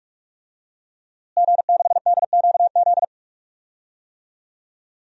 Morse code, heard as a keyed tone, about 43 wpm, 700 Hz. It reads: G6DQZ
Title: DXCC-100 Call Signs - 5 Characters - Repeated 2x - 45wpm